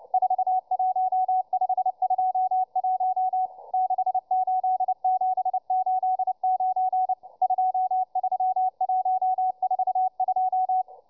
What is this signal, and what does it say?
Morse code, heard as a keyed tone, about 29 wpm, 750 Hz, with radio noise. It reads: 41521 68789 23142